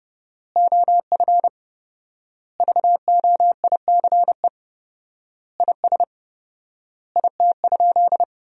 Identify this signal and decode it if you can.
Morse code, heard as a keyed tone, about 32 wpm, 700 Hz. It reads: OF VOICE IS IT?
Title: 4x Word-Spacing - Sentences from Top 500 Words - 30wpm